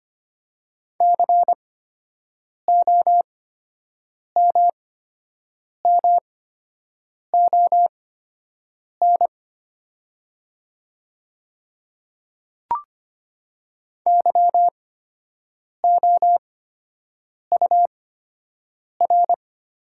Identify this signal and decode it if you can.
Morse code, heard as a keyed tone, about 25 wpm, spaced as for 7 wpm, 700 Hz. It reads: COMMON YOUR
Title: 8x Character Spacing - Top 500 Words - 25wpm